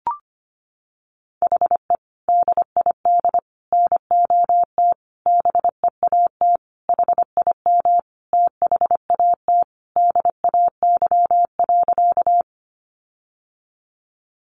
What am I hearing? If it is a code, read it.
Morse code, heard as a keyed tone, about 25 wpm, 700 Hz. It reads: HE DID NOT BEAT HIM THAT DAY.
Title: Sentences from Top 1300 Words 25wpm